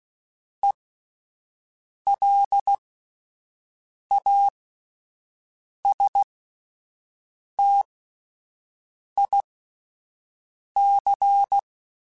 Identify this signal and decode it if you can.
Morse code, heard as a keyed tone, about 16 wpm, spaced as for 5 wpm, 800 Hz. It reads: ELASTIC